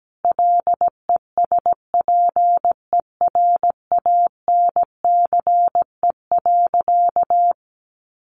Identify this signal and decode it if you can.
Morse code, heard as a keyed tone, about 17 wpm, 700 Hz. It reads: LESPERANCE.
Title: A Sound Of Thunder 17wpm